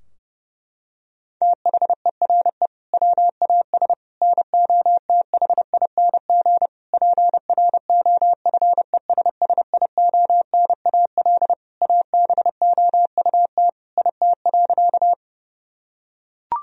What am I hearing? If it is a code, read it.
Morse code, heard as a keyed tone, about 30 wpm, 700 Hz. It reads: THERE WAS NOTHING PROFESSIONAL ABOUT IT.
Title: Sentences from Top 1000 Words - Review Part 4 - 30wpm